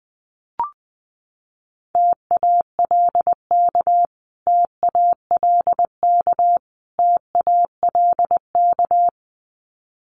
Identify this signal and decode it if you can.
Morse code, heard as a keyed tone, about 20 wpm, 700 Hz. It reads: TALK TALK TALK
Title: Top 500 Words Repeated 20wpm